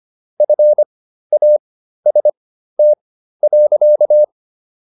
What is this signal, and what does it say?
Morse code, heard as a keyed tone, about 25 wpm, spaced as for 13 wpm, 600 Hz. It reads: FAST.